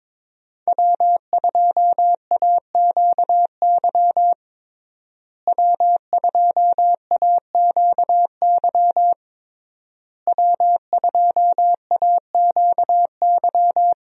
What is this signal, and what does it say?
Morse code, heard as a keyed tone, about 22 wpm, 700 Hz. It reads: W2AQY W2AQY W2AQY
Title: USA Call Signs - 5 Characters - Repeated 3x - 22wpm